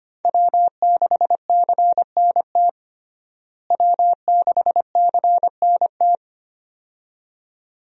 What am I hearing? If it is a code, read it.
Morse code, heard as a keyed tone, about 25 wpm, 700 Hz. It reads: W6CNT W6CNT
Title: USA Call Signs - 5 Characters - Repeated 2x - 25wpm